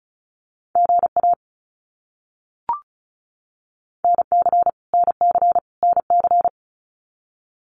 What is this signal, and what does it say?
Morse code, heard as a keyed tone, about 34 wpm, 700 Hz. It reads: GA NC NC NC
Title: US States Repeated 3x 35wpm